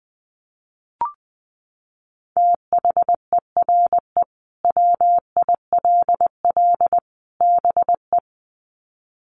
Morse code, heard as a keyed tone, about 19 wpm, 700 Hz. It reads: THERE WILL BE
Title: Sets of 3 Words 20wpm